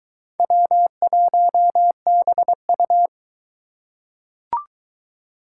Morse code, heard as a keyed tone, about 22 wpm, 700 Hz. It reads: W1BU